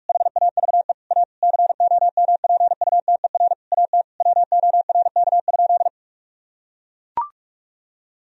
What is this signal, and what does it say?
Morse code, heard as a keyed tone, about 41 wpm, 700 Hz. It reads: HAVE A COMPUTER AT WORK?
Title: Sentences from Top 1000 Words - Review Part 3 - 45wpm